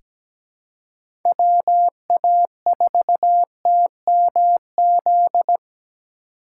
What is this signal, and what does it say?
Morse code, heard as a keyed tone, about 17 wpm, 700 Hz. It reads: WA4TMZ
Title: USA Call Signs - 3x Character Spacing - 17wpm